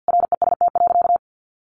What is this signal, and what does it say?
Morse code, heard as a keyed tone, about 49 wpm, 700 Hz. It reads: REST.
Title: Sentences from Top 600 Words 50wpm